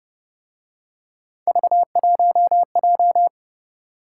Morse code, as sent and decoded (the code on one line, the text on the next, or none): ...- .---- .---
V1J